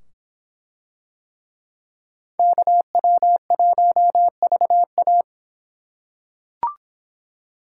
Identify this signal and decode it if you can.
Morse code, heard as a keyed tone, about 23 wpm, 700 Hz. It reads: KW1VA